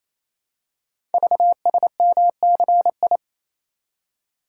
Morse code, heard as a keyed tone, about 27 wpm, 700 Hz. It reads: VSMCI
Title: Code Groups - 5 Letter:Numbers - 28wpm